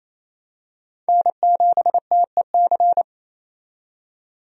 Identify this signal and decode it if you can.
Morse code, heard as a keyed tone, about 27 wpm, 700 Hz. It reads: N7TEC
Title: USA Call Signs - 5 Characters - 28wpm